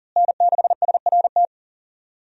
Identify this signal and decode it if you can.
Morse code, heard as a keyed tone, about 40 wpm, 700 Hz. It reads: N6SRT
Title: USA Call Signs - 5 Characters - Repeated 2x - 40wpm